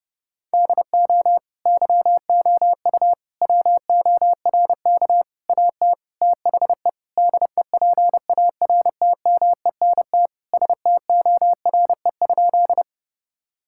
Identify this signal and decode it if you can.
Morse code, heard as a keyed tone, about 32 wpm, 700 Hz. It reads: DO YOU WORK AT THE DEPARTMENT STORE?
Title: Sentences from Top 1000 Words - Review Part 5 - 30wpm